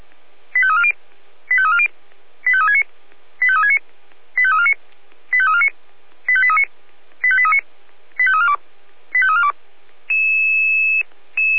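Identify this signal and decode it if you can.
Five-tone selective call (ZVEI), heard as quick runs of five tones, tones of about 70 ms. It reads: 85340 85340 85369 85369 85349 85349 85839 85839 85342 85342
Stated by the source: Received by SDR, ZVEI five-tone call 85330.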